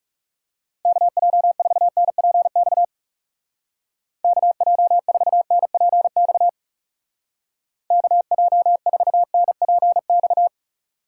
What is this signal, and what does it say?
Morse code, heard as a keyed tone, about 36 wpm, 700 Hz. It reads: KJ4NPX KJ4NPX KJ4NPX
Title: Speed Racing - USA 2x3 Call Signs 30wpm